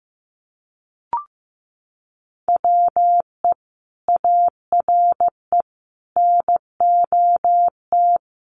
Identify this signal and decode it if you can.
Morse code, heard as a keyed tone, about 15 wpm, 700 Hz. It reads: WE ARE NOT
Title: Sentences from Top 1000 Words 15wpm